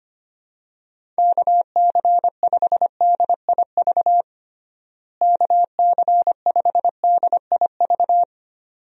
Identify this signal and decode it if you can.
Morse code, heard as a keyed tone, about 25 wpm, 700 Hz. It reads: KC5DIV KC5DIV
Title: USA Call Signs - 6 Characters - Repeated 2x - 25wpm